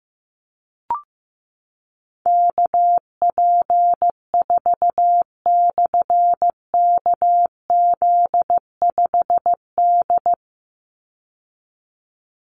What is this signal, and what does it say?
Morse code, heard as a keyed tone, about 15 wpm, 700 Hz. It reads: KP4/KZ5D
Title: Call Signs Repeated 2x - 15wpm